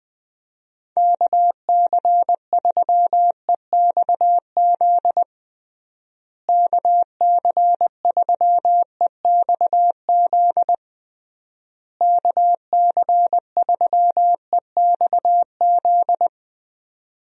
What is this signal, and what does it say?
Morse code, heard as a keyed tone, about 20 wpm, 700 Hz. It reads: KC3EXZ KC3EXZ KC3EXZ
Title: USA Call Signs - 6 Characters - Repeated 3x - 20wpm